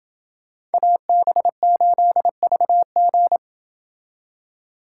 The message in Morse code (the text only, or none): AB8VG